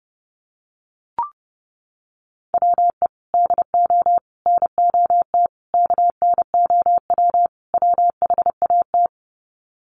Morse code, heard as a keyed tone, about 30 wpm, 700 Hz. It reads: WE DO NOT KNOW WHAT
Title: Sets of 5 Words 30wpm